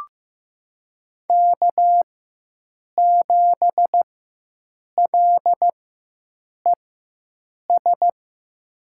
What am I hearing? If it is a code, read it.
Morse code, heard as a keyed tone, about 15 wpm, spaced as for 7 wpm, 700 Hz. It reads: K7LES
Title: USA Call Signs - 5 Characters - 4x Character Spacing - 15wpm